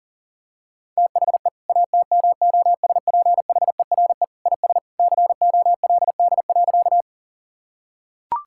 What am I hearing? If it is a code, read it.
Morse code, heard as a keyed tone, about 40 wpm, 700 Hz. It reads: THE ATMOSPHERE IS COLD.
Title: Sentences from Top 1600 Words 40wpm